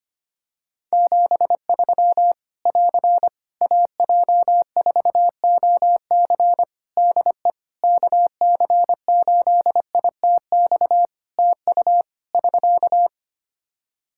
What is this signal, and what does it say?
Morse code, heard as a keyed tone, about 25 wpm, 700 Hz. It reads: 73 <AR> AJ4OC DE KC8ITX TU <SK>
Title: Practice QSO - Exchange 4 - Ending - 25wpm